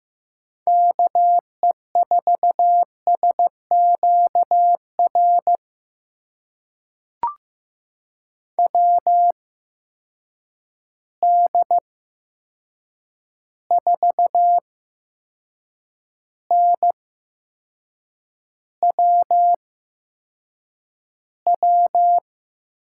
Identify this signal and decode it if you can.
Morse code, heard as a keyed tone, about 15 wpm, spaced as for 6 wpm, 700 Hz. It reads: KE4SQR WD4NWW
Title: USA Call Signs - 6 Characters - 8x Character Spacing - 15wpm